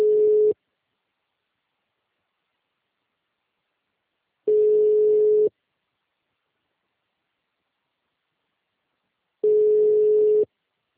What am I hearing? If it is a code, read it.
Ringback tone (Europe, 425 Hz).